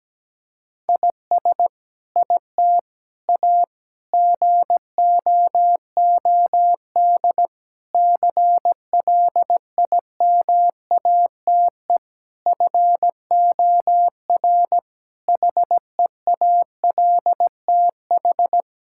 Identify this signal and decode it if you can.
Morse code, heard as a keyed tone, about 17 wpm, 700 Hz. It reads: IS IT A GOOD CLIMATE FOR HEALTH
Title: Sentences from Top 1600 Words 17wpm